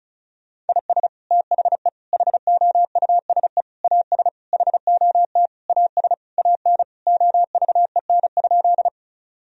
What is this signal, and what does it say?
Morse code, heard as a keyed tone, about 34 wpm, 700 Hz. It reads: IS THE HOUSE AS HOT AS AN OVEN?
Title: Sentences from Top 1000 Words - Review Part 4 - 35wpm